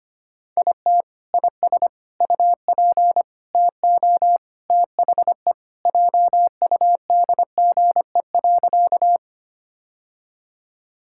Morse code, heard as a keyed tone, about 25 wpm, 700 Hz. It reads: IT IS UP TO THE JUDGE.
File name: Sentences with 3-Word Phrases - 25wpm